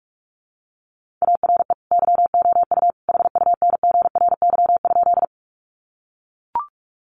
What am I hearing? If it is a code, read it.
Morse code, heard as a keyed tone, about 44 wpm, 700 Hz. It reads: ARE YOU HUNGRY?
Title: Sentences from Top 1400 Words 45wpm